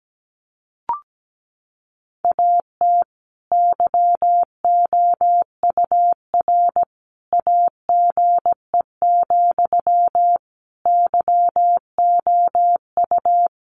Morse code, heard as a keyed tone, about 17 wpm, 700 Hz. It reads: AT YOUR AGE, YOU
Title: Sentences from Top 1500 Words 17wpm